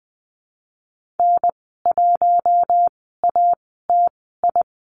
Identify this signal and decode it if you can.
Morse code, heard as a keyed tone, about 20 wpm, spaced as for 14 wpm, 700 Hz. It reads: N1ATI
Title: 122 Lesson 20g - Number 3 - Call Signs 20wpm